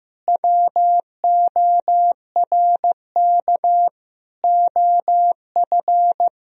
Morse code, heard as a keyed tone, about 15 wpm, 700 Hz. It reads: WORK OF
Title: Sentences from Top 1000 Words - Review Part 3 - 15wpm